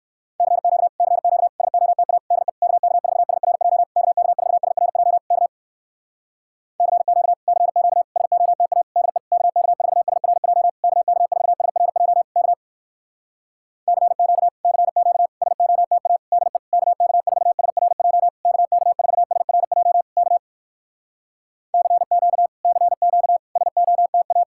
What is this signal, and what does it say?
Morse code, heard as a keyed tone, about 59 wpm, 700 Hz. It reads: CQ CQ SOTA DE KK4SRJ KK4SRJ K CQ CQ SOTA DE KK4SRJ KK4SRJ K CQ CQ SOTA DE KK4SRJ KK4SRJ K CQ CQ SOTA